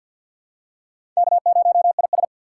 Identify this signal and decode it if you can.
Morse code, heard as a keyed tone, about 48 wpm, 700 Hz. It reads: K0IS
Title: USA Call Signs - 4x Character Spacing - 50wpm